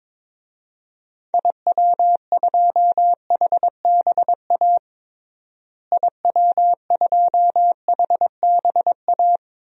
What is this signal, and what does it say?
Morse code, heard as a keyed tone, about 22 wpm, 700 Hz. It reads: IW2HBA IW2HBA
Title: DXCC-100 Call Signs - 6 Characters - Repeated 3x - 22wpm